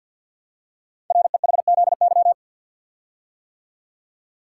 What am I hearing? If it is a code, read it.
Morse code, heard as a keyed tone, about 48 wpm, 700 Hz. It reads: REHZY
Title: Code Groups - 5 Letters - 50wpm